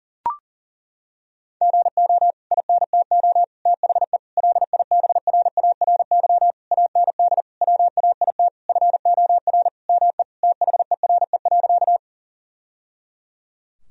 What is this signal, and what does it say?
Morse code, heard as a keyed tone, about 40 wpm, 700 Hz. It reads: GO INTO THE LIBRARY AND WAIT FOR ME THERE.
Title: Sentences from Top 1000 Words - Review Part 4 - 40wpm